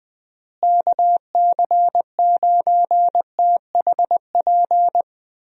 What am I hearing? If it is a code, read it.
Morse code, heard as a keyed tone, about 20 wpm, 700 Hz. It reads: KC9THP
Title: USA Call Signs - 6 Characters - Repeated 2x - 20wpm